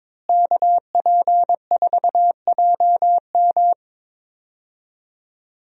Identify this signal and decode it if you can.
Morse code, heard as a keyed tone, about 22 wpm, 700 Hz. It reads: KP4JM
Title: DXCC-100 Call Signs - 5 Characters - Repeated 3x - 22wpm